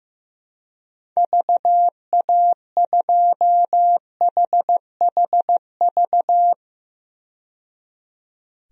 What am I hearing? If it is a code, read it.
Morse code, heard as a keyed tone, about 15 wpm, 700 Hz. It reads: VA2HHV
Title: DXCC-100 Call Signs - 6 Characters - 15wpm